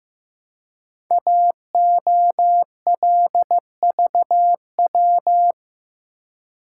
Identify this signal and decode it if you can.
Morse code, heard as a keyed tone, about 15 wpm, 700 Hz. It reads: AOLVW
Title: Code Groups - 5 Letters - 15wpm